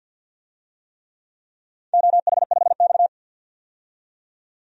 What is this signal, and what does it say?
Morse code, heard as a keyed tone, about 53 wpm, 700 Hz. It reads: OH5X
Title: DXCC-100 Call Signs - 4 Characters - 50wpm